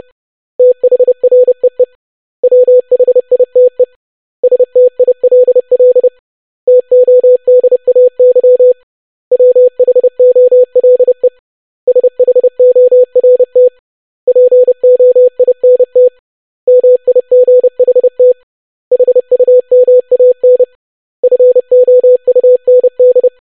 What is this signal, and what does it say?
Morse code, heard as a keyed tone, about 30 wpm, 500 Hz. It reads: THREE WHITE STILL TODAY WHOLE SHORT POINT MIGHT HUMAN FOUND
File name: basic500-25wpm